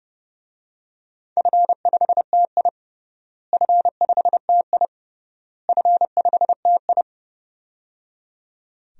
Morse code, heard as a keyed tone, about 30 wpm, 700 Hz. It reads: F5TI F5TI F5TI
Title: DXCC-100 Call Signs - 4 Characters - Repeated 3x - 30wpm